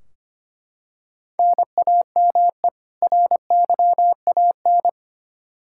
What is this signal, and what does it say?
Morse code, heard as a keyed tone, about 25 wpm, 700 Hz. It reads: NAME RYAN
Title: K1USN SST QSO Practice 25wpm